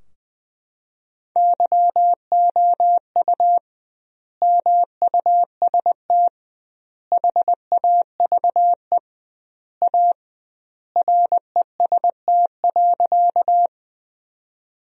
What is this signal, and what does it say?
Morse code, heard as a keyed tone, about 20 wpm, 700 Hz. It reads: YOU MUST HAVE A REST.
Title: 2x Word-Spacing - Sentences from Top 500 Words - 20wpm